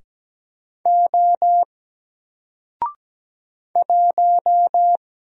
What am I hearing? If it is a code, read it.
Morse code, heard as a keyed tone, about 17 wpm, 700 Hz. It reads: O 1
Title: Single Letter-Number 17wpm